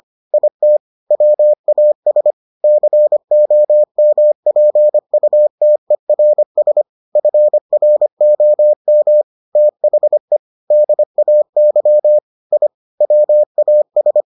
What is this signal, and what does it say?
Morse code, heard as a keyed tone, about 25 wpm, 600 Hz. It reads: IT WAS COMPUTERS FROM THE DAY I WAS